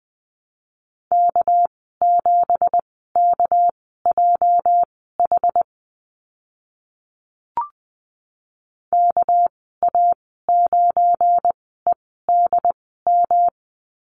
Morse code, heard as a keyed tone, about 20 wpm, spaced as for 14 wpm, 700 Hz. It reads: K7KJH KA9EDM
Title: 219 Lesson 35g - Number 0 - Call Signs 20wpm